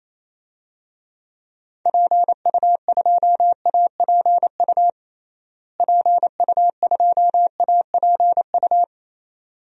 Morse code, heard as a keyed tone, about 28 wpm, 700 Hz. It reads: PU2APU PU2APU
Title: DXCC-100 Call Signs - 6 Characters - Repeated 2x - 28wpm